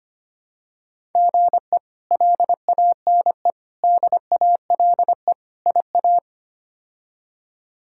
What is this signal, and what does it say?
Morse code, heard as a keyed tone, about 25 wpm, 700 Hz. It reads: GE LANE DALE IA
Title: K1USN SST QSO Practice - 25wpm